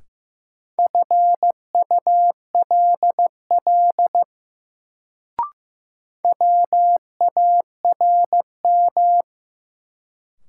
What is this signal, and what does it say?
Morse code, heard as a keyed tone, about 15 wpm, 700 Hz. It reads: FULL WARM
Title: Sentences from Top 500 Words - New Word Drill - 15wpm